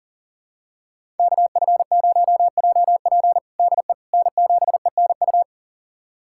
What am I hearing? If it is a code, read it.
Morse code, heard as a keyed tone, about 40 wpm, 700 Hz. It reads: KF0JP DE N7ENU
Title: USA Callsign DE Callsign - 5 Characters - 40wpm